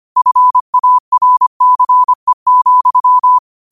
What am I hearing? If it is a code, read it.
Morse code, heard as a keyed tone, about 25 wpm, 1000 Hz. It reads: FARCE,
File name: qotdfiles / 25wpm